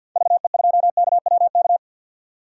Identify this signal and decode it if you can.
Morse code, heard as a keyed tone, about 47 wpm, 700 Hz. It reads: VE2KWK
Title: DXCC-100 Call Signs - 6 Characters - 50wpm